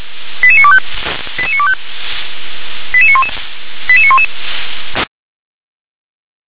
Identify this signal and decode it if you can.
Five-tone selective call (ZVEI), heard as quick runs of five tones, tones of about 70 ms. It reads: 88925 88925 88911 88911